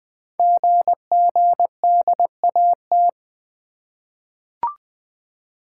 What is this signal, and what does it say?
Morse code, heard as a keyed tone, about 20 wpm, 700 Hz. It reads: GGDAT
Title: Code Groups - 5 Letters - 20wpm